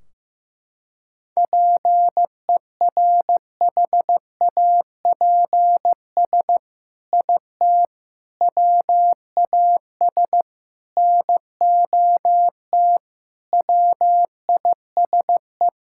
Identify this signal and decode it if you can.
Morse code, heard as a keyed tone, about 15 wpm, 700 Hz. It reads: PERHAPS IT WAS NOT WISE
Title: Sentences from Top 1600 Words 15wpm